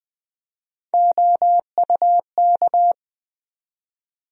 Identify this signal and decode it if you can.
Morse code, heard as a keyed tone, about 20 wpm, 700 Hz. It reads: OUK
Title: USA Call Sign Postfixes - 3 Letters 20wpm